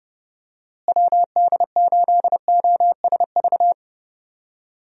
Morse code, heard as a keyed tone, about 32 wpm, 700 Hz. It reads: WD8OSV
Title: USA Call Signs - 6 Characters - Repeated 3x - 30wpm